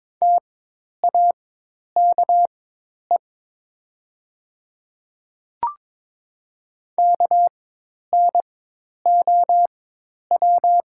Morse code, heard as a keyed tone, about 22 wpm, spaced as for 10 wpm, 700 Hz. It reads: TAKE KNOW
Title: 4x Character Spacing - Top 300 Words - 22wpm